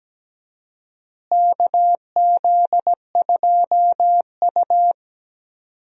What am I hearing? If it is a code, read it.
Morse code, heard as a keyed tone, about 17 wpm, 700 Hz. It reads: KZ2U
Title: USA Call Signs - 4 Characters - Repeated 2x - 17wpm